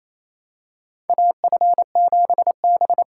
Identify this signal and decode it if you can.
Morse code, heard as a keyed tone, about 28 wpm, 700 Hz. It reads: AF7B
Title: USA Call Signs - 4 Characters - 3x Character Spacing - 28wpm